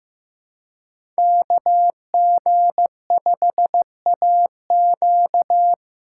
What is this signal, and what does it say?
Morse code, heard as a keyed tone, about 18 wpm, 700 Hz. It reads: KG5AQ
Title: Speed Racing - USA 2x2 Call Signs 15wpm